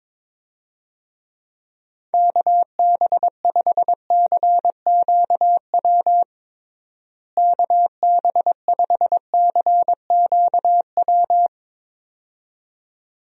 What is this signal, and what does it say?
Morse code, heard as a keyed tone, about 21 wpm, 700 Hz. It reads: KB5CQW KB5CQW
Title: USA Call Signs - 6 Characters - Repeated 2x - 22wpm